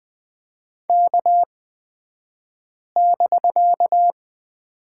Morse code, heard as a keyed tone, about 20 wpm, spaced as for 5 wpm, 700 Hz. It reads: K<BK>
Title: 249 Lesson 40d - BK - Two Characters 20wpm